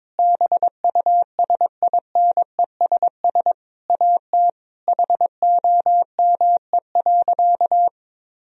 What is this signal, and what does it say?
Morse code, heard as a keyed tone, about 21 wpm, 700 Hz. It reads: BUSINESS AT HOME.
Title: Sentences from Top 1000 Words - Review Part 4 - 22wpm